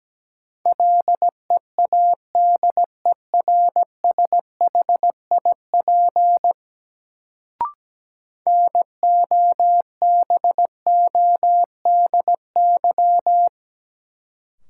Morse code, heard as a keyed tone, about 17 wpm, 700 Hz. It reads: LEADERSHIP NOBODY
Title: Sentences from Top 1600 Words - New Word Drill - 17wpm